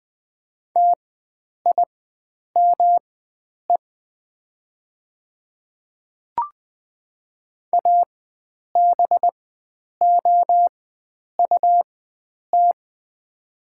Morse code, heard as a keyed tone, about 20 wpm, spaced as for 9 wpm, 700 Hz. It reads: TIME ABOUT
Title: 4x Character Spacing - Top 100 Words - 20wpm